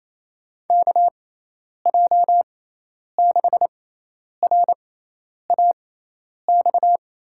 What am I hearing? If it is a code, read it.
Morse code, heard as a keyed tone, about 28 wpm, spaced as for 10 wpm, 700 Hz. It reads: KJ6RAX
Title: USA Call Signs - 6 Characters - 6x Character Spacing - 28wpm